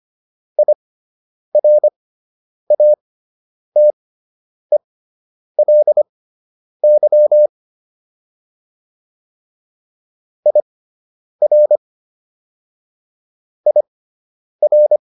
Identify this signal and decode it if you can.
Morse code, heard as a keyed tone, about 25 wpm, spaced as for 9 wpm, 600 Hz. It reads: IRATELY IR IR